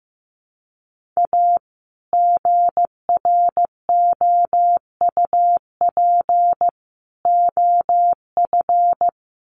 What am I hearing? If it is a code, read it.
Morse code, heard as a keyed tone, about 15 wpm, 700 Hz. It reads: A GROUP OF
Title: Sets of 3 Words 15wpm